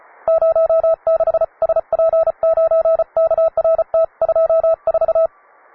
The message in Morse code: 06 SP9KRT 24